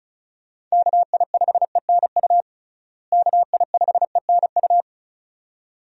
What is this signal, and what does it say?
Morse code, heard as a keyed tone, about 36 wpm, 700 Hz. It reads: KI5ENU KI5ENU
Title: USA Call Signs - 6 Characters - Repeated 2x - 35wpm